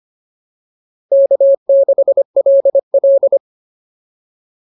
Morse code, heard as a keyed tone, about 25 wpm, 560 Hz. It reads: K6LL